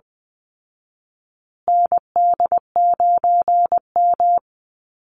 Morse code, heard as a keyed tone, about 20 wpm, 700 Hz. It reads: ND9M
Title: Call Signs Repeated 2x - 20wpm